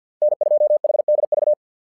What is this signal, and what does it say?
Morse code, heard as a keyed tone, about 50 wpm, 600 Hz. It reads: N2HDV
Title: SOTA QSO Practice 50wpm